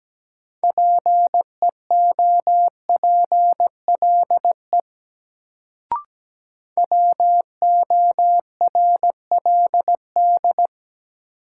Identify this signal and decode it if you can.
Morse code, heard as a keyed tone, about 17 wpm, 700 Hz. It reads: PEOPLE WORLD
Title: Sentences from Top 200 Words - New Word Drill - 17wpm